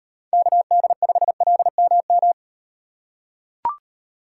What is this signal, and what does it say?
Morse code, heard as a keyed tone, about 32 wpm, 700 Hz. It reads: KD5LMM